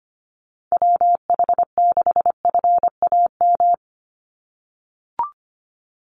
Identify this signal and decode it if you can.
Morse code, heard as a keyed tone, about 25 wpm, 700 Hz. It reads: WH6FAM
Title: USA Call Signs 25wpm